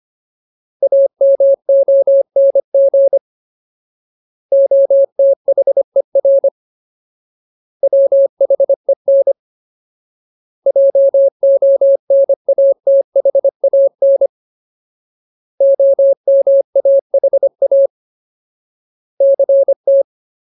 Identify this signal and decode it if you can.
Morse code, heard as a keyed tone, about 25 wpm, 550 Hz. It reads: AMONG OTHER WHEN JONATHAN OMAHA CT
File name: Ultimate Word List / 25WPM Ultimate